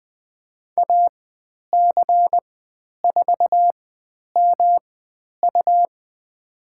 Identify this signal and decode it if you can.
Morse code, heard as a keyed tone, about 20 wpm, spaced as for 10 wpm, 700 Hz. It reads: AC4MU
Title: 109 Lesson 18g - Letter m - Call Signs 20wpm